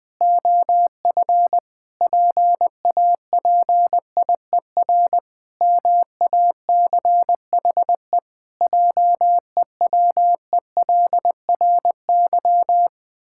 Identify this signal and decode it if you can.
Morse code, heard as a keyed tone, about 20 wpm, 700 Hz. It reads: OF PAPIER MACHE JEWELRY
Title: Pangram Challenge v2 - 20wpm